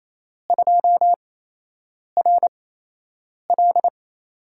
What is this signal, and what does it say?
Morse code, heard as a keyed tone, about 28 wpm, spaced as for 8 wpm, 700 Hz. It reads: 2RL
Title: USA Call Signs - 4 Characters - 8x Character Spacing - 28wpm